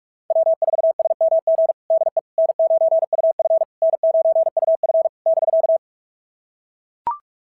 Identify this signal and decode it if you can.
Morse code, heard as a keyed tone, about 41 wpm, 690 Hz. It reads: W4SMG DE N9UF N9UF <BK>
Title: SOTA QSO Practice 45wpm